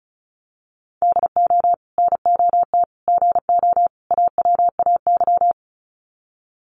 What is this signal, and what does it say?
Morse code, heard as a keyed tone, about 34 wpm, 700 Hz. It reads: DO NOT GO AWAY
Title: Sets of 4 Words 35wpm